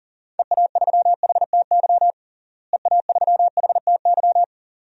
Morse code, heard as a keyed tone, about 40 wpm, 700 Hz. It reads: EA3HTY EA3HTY
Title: DXCC-100 Call Signs - 6 Characters - Repeated 2x - 40wpm